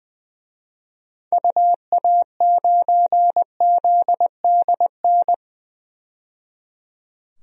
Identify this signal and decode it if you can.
Morse code, heard as a keyed tone, about 21 wpm, 700 Hz. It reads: UA9ZDN
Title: DXCC-100 Call Signs - 6 Characters - 20wpm